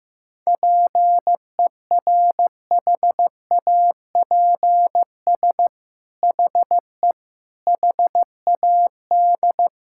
Morse code, heard as a keyed tone, about 15 wpm, 700 Hz. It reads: PERHAPS HE HAD
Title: Sentences from Top 1600 Words 15wpm